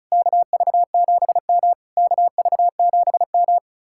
Morse code, heard as a keyed tone, about 46 wpm, 700 Hz. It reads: KV7M KV7M